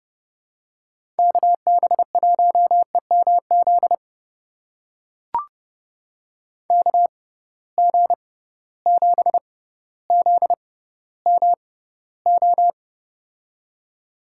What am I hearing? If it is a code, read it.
Morse code, heard as a keyed tone, about 32 wpm, spaced as for 16 wpm, 700 Hz. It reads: KB1EMZ KG7ZMO
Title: USA Call Signs - 6 Characters - 6x Character Spacing - 30wpm